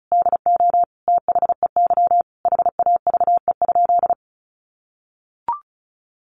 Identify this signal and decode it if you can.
Morse code, heard as a keyed tone, about 34 wpm, 700 Hz. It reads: DO THEY HAVE?
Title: Sentences from Top 1200 Words 35wpm